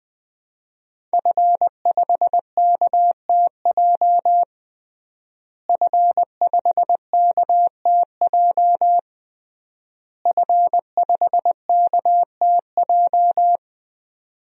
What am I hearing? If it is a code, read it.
Morse code, heard as a keyed tone, about 20 wpm, 700 Hz. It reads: F5KTJ F5KTJ F5KTJ
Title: DXCC-100 Call Signs - 5 Characters - Repeated 3x - 20wpm